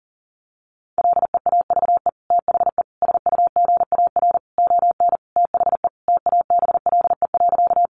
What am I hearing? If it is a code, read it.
Morse code, heard as a keyed tone, about 37 wpm, 700 Hz. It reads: LEAVE THE SUGAR ON THE TABLE.
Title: Sentences from Top 1200 Words 40wpm